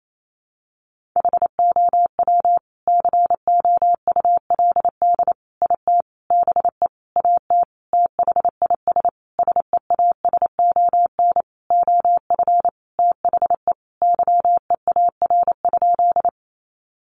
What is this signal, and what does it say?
Morse code, heard as a keyed tone, about 28 wpm, 700 Hz. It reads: HOW COULD IT BE AT THIS SEASON OF THE YEAR?
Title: Sentences from Top 1400 Words 28wpm